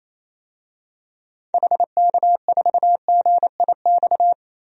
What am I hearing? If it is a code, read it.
Morse code, heard as a keyed tone, about 28 wpm, 700 Hz. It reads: HK4GIX
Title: DXCC-100 Call Signs - 6 Characters - 28wpm